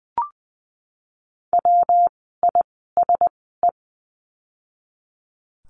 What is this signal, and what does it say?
Morse code, heard as a keyed tone, about 20 wpm, spaced as for 15 wpm, 700 Hz. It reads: WISE
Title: 115 Lesson 19f - Letter w - Words 20wpm